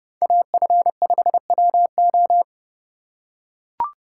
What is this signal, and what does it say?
Morse code, heard as a keyed tone, about 28 wpm, 700 Hz. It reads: AF5WO